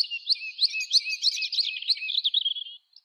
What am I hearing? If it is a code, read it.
There is no code.